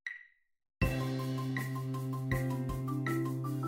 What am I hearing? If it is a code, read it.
There is no code.